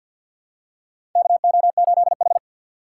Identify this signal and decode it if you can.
Morse code, heard as a keyed tone, about 53 wpm, 700 Hz. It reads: KO8H
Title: USA Call Signs - 4x Character Spacing - 50wpm